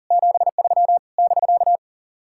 Morse code, heard as a keyed tone, about 50 wpm, 700 Hz. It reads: 73 <BK>
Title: Speed-Racing Common QSO Phrases - 40wpm